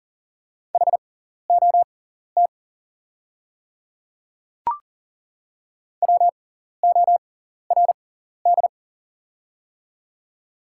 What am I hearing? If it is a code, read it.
Morse code, heard as a keyed tone, about 39 wpm, spaced as for 14 wpm, 700 Hz. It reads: HOT WORD
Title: 6x Character Spacing - Top 100 Words - 40wpm